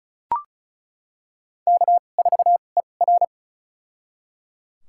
Morse code, heard as a keyed tone, about 34 wpm, spaced as for 25 wpm, 700 Hz. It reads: K4ER
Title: USA Call Signs - 4 Characters - 2x Character Spacing - 35wpm